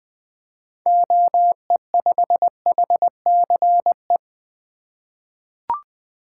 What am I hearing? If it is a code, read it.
Morse code, heard as a keyed tone, about 21 wpm, 700 Hz. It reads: OE5HCE
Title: 2023 SOTA Activator Call Signs - 20wpm